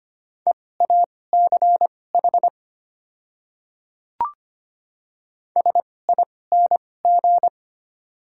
Morse code, heard as a keyed tone, about 25 wpm, spaced as for 18 wpm, 700 Hz. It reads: EACH SING
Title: 2x Character Spacing - Top 500 Words - 25wpm